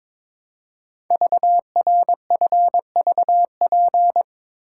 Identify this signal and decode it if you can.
Morse code, heard as a keyed tone, about 22 wpm, 700 Hz. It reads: VRFVP
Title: Code Groups - 5 Letters - 22wpm